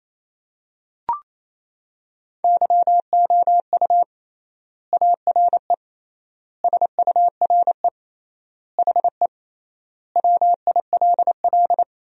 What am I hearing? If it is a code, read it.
Morse code, heard as a keyed tone, about 28 wpm, 700 Hz. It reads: YOU ARE SURE HE WILL
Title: 3x Word-Spacing - Sentences from Top 300 Words - 28wpm